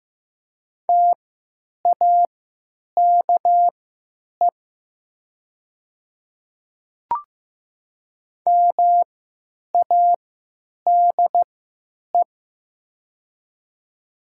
Morse code, heard as a keyed tone, about 15 wpm, spaced as for 9 wpm, 700 Hz. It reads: TAKE MADE
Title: 3x Character Spacing - Top 200 Words - 15wpm